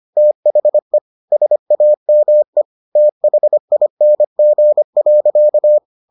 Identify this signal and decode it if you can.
Morse code, heard as a keyed tone, about 25 wpm, 600 Hz. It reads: THE SAME THING.